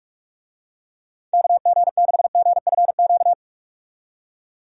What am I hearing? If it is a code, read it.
Morse code, heard as a keyed tone, about 43 wpm, 700 Hz. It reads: KG6GFQ
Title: USA Call Signs - 6 Characters - 45wpm